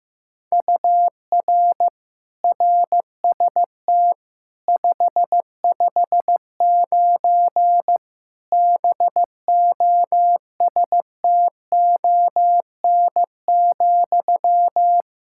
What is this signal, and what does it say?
Morse code, heard as a keyed tone, about 15 wpm, 700 Hz. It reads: UR RST 559 BOSTON,
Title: Practice QSO - Exchange 1 - Beginning - 15wpm